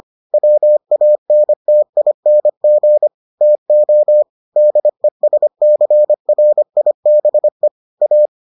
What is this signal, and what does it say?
Morse code, heard as a keyed tone, about 25 wpm, 600 Hz. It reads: WANTING TO DESCRIBE A